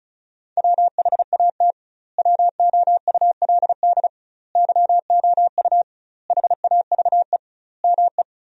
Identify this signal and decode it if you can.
Morse code, heard as a keyed tone, about 36 wpm, 700 Hz. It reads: WHAT WOULD YOU HAVE ME
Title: 2x Word-Spacing - Sentences from Top 200 Words - 35wpm